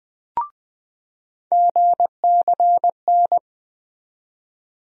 Morse code, heard as a keyed tone, about 20 wpm, 700 Hz. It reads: GCN
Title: USA Call Sign Postfixes - 3 Letters 20wpm